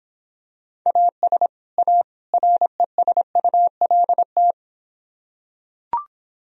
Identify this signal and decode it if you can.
Morse code, heard as a keyed tone, about 26 wpm, 700 Hz. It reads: AS A RESULT